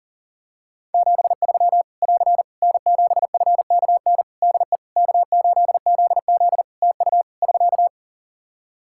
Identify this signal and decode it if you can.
Morse code, heard as a keyed tone, about 40 wpm, 700 Hz. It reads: 73 <AR> N7FKN DE K8ZZ TU <SK>
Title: Practice QSO - Exchange 4 - Ending - 40wpm